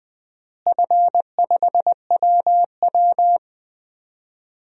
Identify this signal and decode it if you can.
Morse code, heard as a keyed tone, about 21 wpm, 700 Hz. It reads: F5WW
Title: DXCC-100 Call Signs - 4 Characters - Repeated 3x - 20wpm